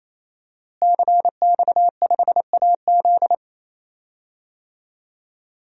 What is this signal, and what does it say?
Morse code, heard as a keyed tone, about 27 wpm, 700 Hz. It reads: CX5AZ
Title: DXCC-100 Call Signs - 5 Characters - Repeated 3x - 28wpm